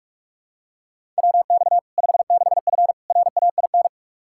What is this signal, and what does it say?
Morse code, heard as a keyed tone, about 46 wpm, 700 Hz. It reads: WX 56F RAIN
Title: Practice QSO - Exchange 2 - Info - 45wpm